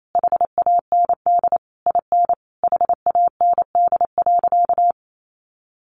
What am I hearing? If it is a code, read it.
Morse code, heard as a keyed tone, about 28 wpm, 700 Hz. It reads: HAND IN HAND.
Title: Sentences from Top 1100 Words 28wpm